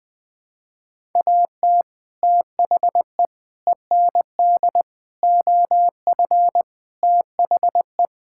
Morse code, heard as a keyed tone, about 20 wpm, 700 Hz. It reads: AT THE END OF THE
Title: Sentences with 3-Word Phrases - Spoken Beginning - 20wpm